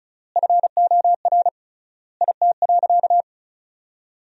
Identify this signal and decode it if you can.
Morse code, heard as a keyed tone, about 36 wpm, 700 Hz. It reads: FOR IT.
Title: 3x Word-Spacing - Sentences from Top 200 Words - 35wpm